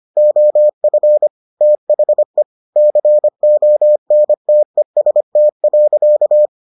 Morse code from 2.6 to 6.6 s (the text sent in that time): CONTEST.